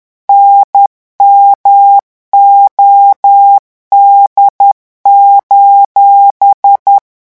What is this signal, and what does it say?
Morse code, heard as a keyed tone, about 10 wpm, 800 Hz. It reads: NMOD:
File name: morse mp3 / revision16-vitesse 10 mots